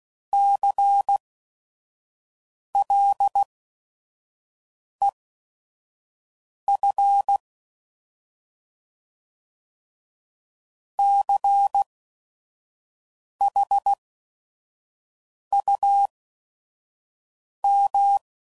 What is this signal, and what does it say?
Morse code, heard as a keyed tone, about 16 wpm, spaced as for 5 wpm, 800 Hz. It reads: CLEF CHUM
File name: Prefixes / By First Letter / Cs